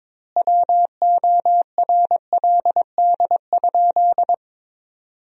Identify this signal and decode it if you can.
Morse code, heard as a keyed tone, about 23 wpm, 700 Hz. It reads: WORLD?
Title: 4x Word-Spacing - Sentences from Top 200 Words - 22wpm